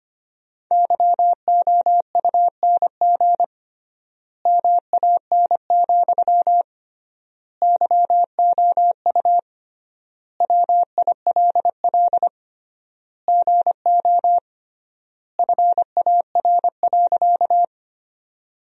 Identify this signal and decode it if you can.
Morse code, heard as a keyed tone, about 25 wpm, 700 Hz. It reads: YOUNG MAN, YOU WILL GO FAR.
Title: 3x Word-Spacing - Sentences from Top 400 Words - 25wpm